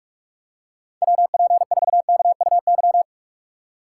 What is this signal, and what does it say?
Morse code, heard as a keyed tone, about 46 wpm, 700 Hz. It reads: WP4KUY
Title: USA Call Signs - 6 Characters - 6x Character Spacing - 45wpm